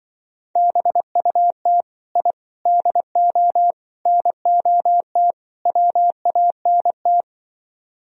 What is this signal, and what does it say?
Morse code, heard as a keyed tone, about 24 wpm, 700 Hz. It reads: BUT I DO NOT WANT
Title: Speed-Racing - Sets of 5 Words 20wpm